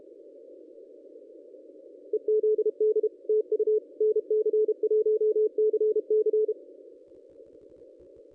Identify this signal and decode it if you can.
Morse code, heard as a keyed tone, about 31 wpm, 440 Hz. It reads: EZD TU NC1CC
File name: contests / ss-cw-k5tr-op-n5rz -1200